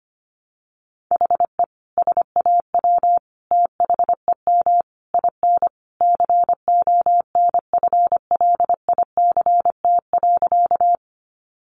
Morse code, heard as a keyed tone, about 25 wpm, 700 Hz. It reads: HE SAW THEM IN CONFLICT.
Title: Sentences from Top 1300 Words 25wpm